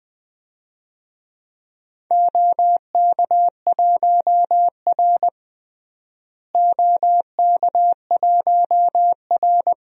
Morse code, heard as a keyed tone, about 20 wpm, 700 Hz. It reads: OK1R OK1R
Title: DXCC-100 Call Signs - 4 Characters - Repeated 2x - 20wpm